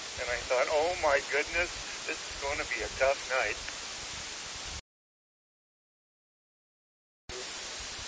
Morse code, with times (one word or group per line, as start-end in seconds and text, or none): none